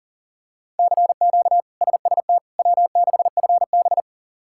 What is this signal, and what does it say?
Morse code, heard as a keyed tone, about 37 wpm, 700 Hz. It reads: CQ SST W6FB
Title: K1USN SST QSO Practice - 40wpm